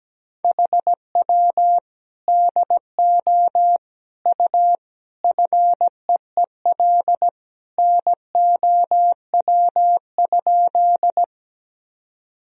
Morse code, heard as a keyed tone, about 17 wpm, 700 Hz. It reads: HW DO U FEEL NOW?
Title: Sentences from Top 400 Words and Abbreviations 17wpm